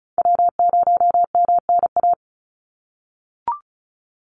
Morse code, heard as a keyed tone, about 34 wpm, 700 Hz. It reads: W0MNA
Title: NA SOTA Chaser Callsigns 35wpm